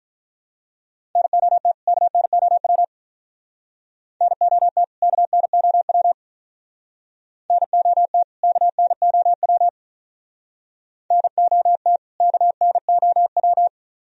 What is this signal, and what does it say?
Morse code, heard as a keyed tone, about 45 wpm, 700 Hz. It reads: NOT KNOW NOT KNOW NOT KNOW NOT KNOW